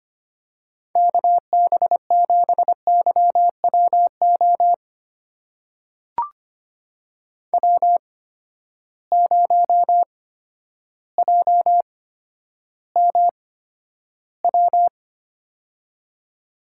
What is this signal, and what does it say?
Morse code, heard as a keyed tone, about 25 wpm, spaced as for 11 wpm, 700 Hz. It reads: KB7YWO W0JMW
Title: USA Call Signs - 8x Character Spacing - 25wpm